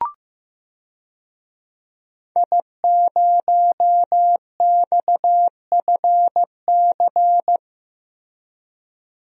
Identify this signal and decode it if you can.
Morse code, heard as a keyed tone, about 15 wpm, 700 Hz. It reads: I0XFC
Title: DXCC-100 Call Signs - 5 Characters - Repeated 3x - 15wpm